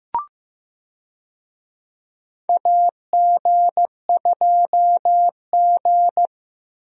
Morse code, heard as a keyed tone, about 16 wpm, 700 Hz. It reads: AG2G
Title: USA Call Signs - 4 Characters - Repeated 3x - 15wpm